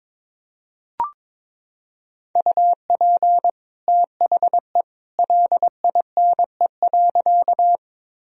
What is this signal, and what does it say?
Morse code, heard as a keyed tone, about 22 wpm, 700 Hz. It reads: UP THE LINE.
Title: Sentences from Top 1000 Words - Review Part 1 - 22wpm